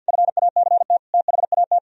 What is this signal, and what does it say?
Morse code, heard as a keyed tone, about 63 wpm, 700 Hz. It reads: FACT THAT